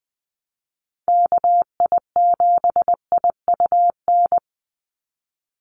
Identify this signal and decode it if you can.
Morse code, heard as a keyed tone, about 20 wpm, 700 Hz. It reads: KI7IUN
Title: USA Call Signs 20wpm